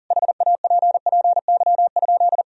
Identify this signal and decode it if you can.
Morse code, heard as a keyed tone, about 40 wpm, 700 Hz. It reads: HAPPY?
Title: Sentences from Top 1000 Words - Review Part 2 - 40wpm